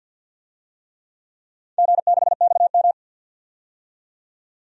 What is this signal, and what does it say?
Morse code, heard as a keyed tone, about 47 wpm, 700 Hz. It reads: G6XM
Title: DXCC-100 Call Signs - 4 Characters - 50wpm